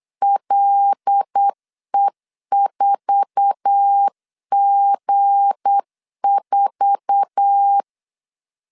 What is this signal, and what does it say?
Morse code, heard as a keyed tone, about 8 wpm, 800 Hz. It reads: LE4G4
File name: CW / DET / 1 9 1500C S B 8wpm 5624